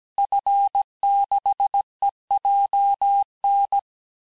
Morse code, heard as a keyed tone, about 17 wpm, 800 Hz. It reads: F6EJN
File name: fin du QSO vitesse 16 mpm